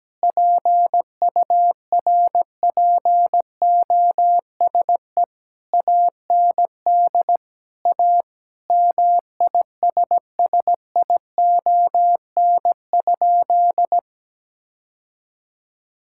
Morse code, heard as a keyed tone, about 17 wpm, 700 Hz. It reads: PURPOSE AND A MISSION?
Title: Sentences from Top 1600 Words 17wpm